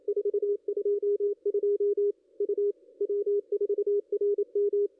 Morse code, heard as a keyed tone, about 27 wpm, 410 Hz. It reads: 422 U W4RM
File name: contests / ss-cw-k5tr-op-n5rz -0200